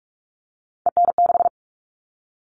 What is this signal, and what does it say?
Morse code, heard as a keyed tone, about 44 wpm, 700 Hz. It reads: EN6
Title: All Cut Numbers - 3-Digit 45wpm